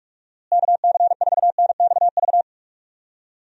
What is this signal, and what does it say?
Morse code, heard as a keyed tone, about 46 wpm, 700 Hz. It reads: KC4NXV
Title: USA Call Signs - 6 Characters - Repeated 3x - 45wpm